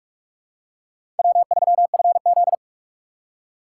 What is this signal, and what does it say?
Morse code, heard as a keyed tone, about 46 wpm, 700 Hz. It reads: W3FZ
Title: USA Call Signs - 4 Characters - 6x Character Spacing - 45wpm